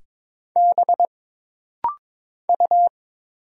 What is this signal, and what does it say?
Morse code, heard as a keyed tone, about 22 wpm, 700 Hz. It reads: B U
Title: Single Letter-Number Alt - Rapid-Fire 22wpm